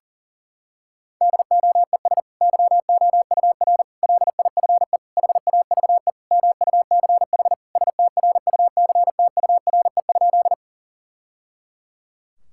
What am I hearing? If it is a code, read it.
Morse code, heard as a keyed tone, about 40 wpm, 700 Hz. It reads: DOES YOUR LIFE HAVE MUCH STRUCTURE?
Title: Sentences from Top 1000 Words - Review Part 6 - 40wpm